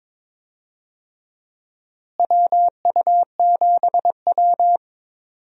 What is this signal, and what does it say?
Morse code, heard as a keyed tone, about 22 wpm, 700 Hz. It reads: WU7W
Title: USA Call Signs - 4 Characters - Repeated 3x - 22wpm